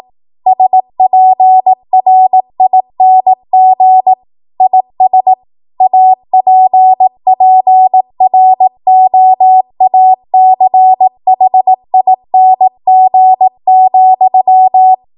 Morse code, heard as a keyed tone, about 18 wpm, 750 Hz. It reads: SPRING IS APPROACHING,